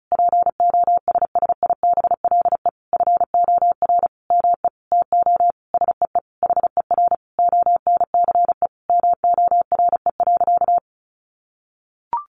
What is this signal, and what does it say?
Morse code, heard as a keyed tone, about 34 wpm, 700 Hz. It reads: POSSIBLE FOR ME TO SEE HER ONCE MORE.
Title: Sentences from Top 500 Words 35wpm